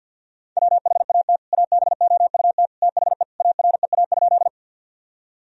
Morse code, heard as a keyed tone, about 51 wpm, 700 Hz. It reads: WHAT ABOUT THE AREA?
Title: Sentences from Top 1000 Words - Review Part 2 - 50wpm